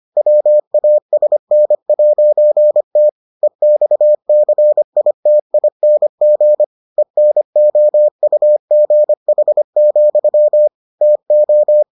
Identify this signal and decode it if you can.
Morse code, heard as a keyed tone, about 25 wpm, 600 Hz. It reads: WASN'T EXCITING ENOUGH, TO